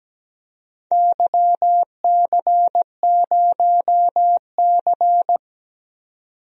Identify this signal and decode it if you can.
Morse code, heard as a keyed tone, about 17 wpm, 700 Hz. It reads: YC0C
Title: DXCC-100 Call Signs - 4 Characters - 17wpm